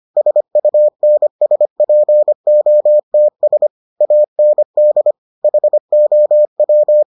Morse code, heard as a keyed tone, about 25 wpm, 600 Hz. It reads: SUNSPOTS AND HOW